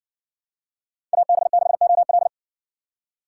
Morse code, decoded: AB6OL

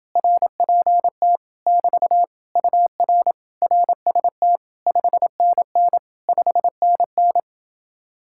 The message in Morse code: RPT <BT> UR RST 5NN 5NN

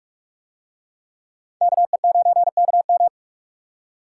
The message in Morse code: KE9KM

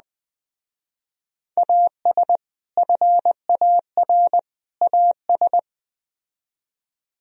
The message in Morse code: AS FAR AS

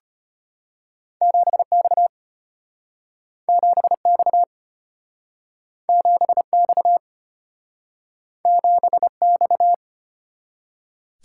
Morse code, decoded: 7X 7X 7X 7X

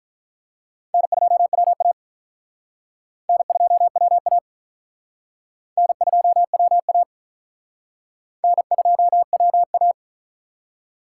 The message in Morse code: N2WA N2WA N2WA N2WA

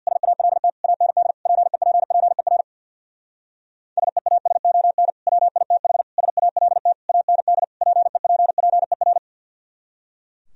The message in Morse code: SALT AND PEPPER SEASON WITH SALT AND PEPPER

